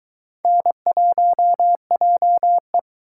Morse code, heard as a keyed tone, about 24 wpm, 700 Hz. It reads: N1JE